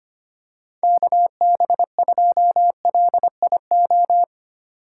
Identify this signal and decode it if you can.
Morse code, heard as a keyed tone, about 25 wpm, 700 Hz. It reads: KB2LIO
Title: USA Call Signs - 6 Characters - 3x Character Spacing - 25wpm